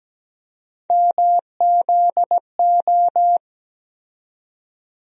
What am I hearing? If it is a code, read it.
Morse code, heard as a keyed tone, about 17 wpm, 700 Hz. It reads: MZO
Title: USA Call Sign Postfixes - 3 Letters 17wpm